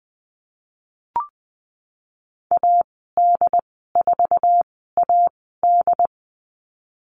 Morse code, heard as a keyed tone, about 19 wpm, spaced as for 14 wpm, 700 Hz. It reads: AD4AD
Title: 088 Lesson 15g - Number 5 - Call Signs 20wpm